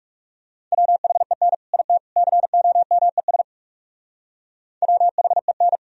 Morse code, heard as a keyed tone, about 37 wpm, 700 Hz. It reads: WHEN IT COMES WHEN
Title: Speed-Racing - Sets of 3 Words 30wpm